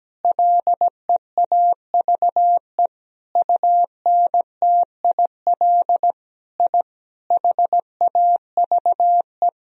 Morse code, heard as a keyed tone, about 17 wpm, 700 Hz. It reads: LEAVE UNTIL I HAVE
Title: Sentences from Top 1000 Words - Review Part 1 - 17wpm